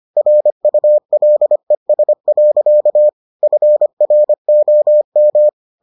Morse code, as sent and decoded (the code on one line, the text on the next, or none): .-. ..- .-.. ... .-.-.- / ..-. .-. --- --
RULES. FROM